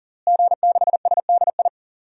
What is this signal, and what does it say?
Morse code, heard as a keyed tone, about 40 wpm, 700 Hz. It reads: G6SDI
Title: DXCC-100 Call Signs - 5 Characters - Repeated 2x - 40wpm